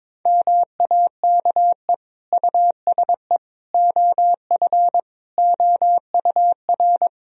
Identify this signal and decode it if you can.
Morse code, heard as a keyed tone, about 22 wpm, 700 Hz. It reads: MAKE USE OF OUR
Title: Sentences from Top 1000 Words - Review Part 4 - 22wpm